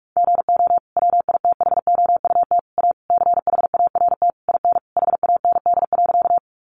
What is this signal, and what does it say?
Morse code, heard as a keyed tone, about 46 wpm, 700 Hz. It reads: GO WITHOUT A CHART IN HAND.
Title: Sentences from Top 1300 Words 45wpm